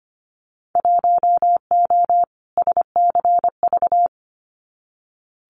Morse code, heard as a keyed tone, about 25 wpm, 700 Hz. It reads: 1O SCV
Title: Winter Field Day Exchanges 25wpm